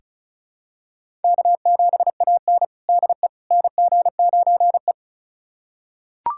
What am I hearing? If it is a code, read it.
Morse code, heard as a keyed tone, about 36 wpm, 700 Hz. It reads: K7AN DE NG9E
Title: USA Callsign DE Callsign - 4 Characters - 35wpm